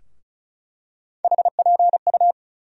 Morse code, heard as a keyed tone, about 34 wpm, 700 Hz. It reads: HPU
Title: USA Call Sign Postfixes - 3 Letters 35wpm